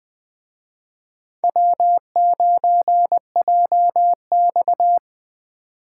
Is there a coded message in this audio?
Yes: Morse code from 1.4 to 5.0 s, heard as a keyed tone, about 20 wpm, 700 Hz. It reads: W9JX